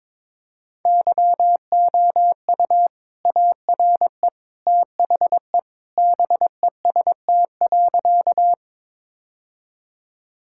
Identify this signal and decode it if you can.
Morse code, heard as a keyed tone, about 22 wpm, 700 Hz. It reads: YOU ARE THE BEST.
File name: practice / 22wpm / Sentences with 2-Word Phrases - 22wpm